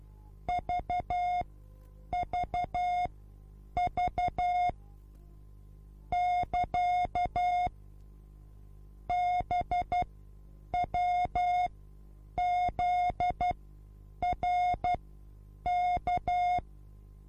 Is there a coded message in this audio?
Yes: Morse code from 0.5 to 16.6 s, heard as a keyed tone, about 12 wpm, spaced as for 8 wpm, 720 Hz. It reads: VVV <KA> BWZRK